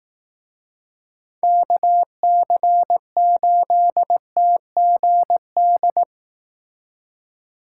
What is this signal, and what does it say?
Morse code, heard as a keyed tone, about 18 wpm, 700 Hz. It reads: KC8TGD
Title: Speed Racing - USA 2x3 Call Signs 15wpm